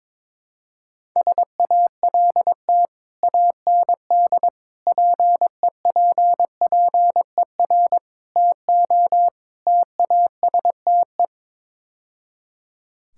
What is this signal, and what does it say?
Morse code, heard as a keyed tone, about 22 wpm, 700 Hz. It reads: SALT AND PEPPER TO TASTE
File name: practice / 22wpm / Build A Phrase - 5-Word Phrases - 22wpm